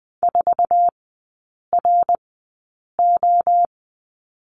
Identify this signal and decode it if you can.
Morse code, heard as a keyed tone, about 20 wpm, spaced as for 8 wpm, 700 Hz. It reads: 4RO
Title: 053 Lesson 10e - Letter r - Three Characters 20wpm